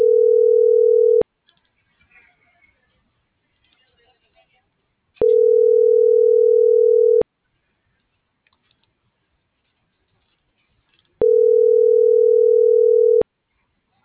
Ringback tone (North America), 440 + 480 Hz.